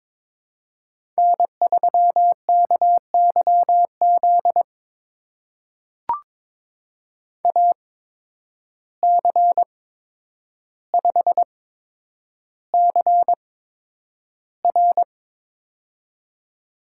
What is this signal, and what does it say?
Morse code, heard as a keyed tone, about 22 wpm, spaced as for 9 wpm, 700 Hz. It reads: N3KYZ AC5CR
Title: USA Call Signs - 5 Characters - 8x Character Spacing - 22wpm